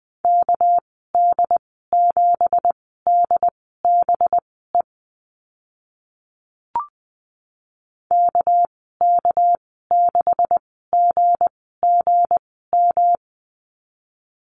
Morse code, heard as a keyed tone, about 20 wpm, spaced as for 15 wpm, 700 Hz. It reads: KD7DBE KK6GGM
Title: 200 Lesson 32g - Letter k - Call Signs 20wpm